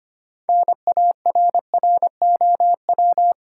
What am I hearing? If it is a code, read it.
Morse code, heard as a keyed tone, about 25 wpm, 700 Hz. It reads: NARROW
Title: Six Letter Words - 25wpm